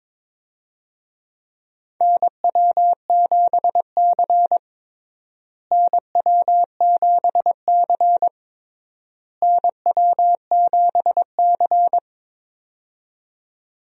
Morse code, heard as a keyed tone, about 22 wpm, 700 Hz. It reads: NW7C NW7C NW7C
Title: USA Call Signs - 4 Characters - Repeated 3x - 22wpm